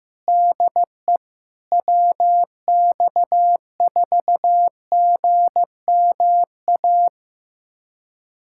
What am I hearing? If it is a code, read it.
Morse code, heard as a keyed tone, about 15 wpm, 700 Hz. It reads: DE WX4GMA
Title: USA Callsign DE Callsign - 6 Characters - 15wpm